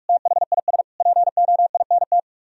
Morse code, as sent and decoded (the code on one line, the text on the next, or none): - .... .. ... / .--. --- .. -. -
THIS POINT